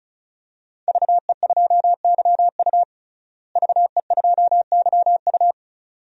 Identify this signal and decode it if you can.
Morse code, heard as a keyed tone, about 34 wpm, 700 Hz. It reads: VE2YU VE2YU
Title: DXCC-100 Call Signs - 5 Characters - Repeated 2x - 35wpm